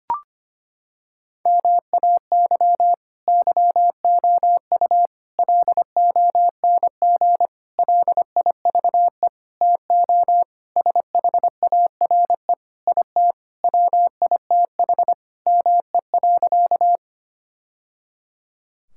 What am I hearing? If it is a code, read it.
Morse code, heard as a keyed tone, about 25 wpm, 700 Hz. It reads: MAY YOU LONG LIVE TO SHARE IT WITH ME.
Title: Sentences from Top 1000 Words - Review Part 6 - 25wpm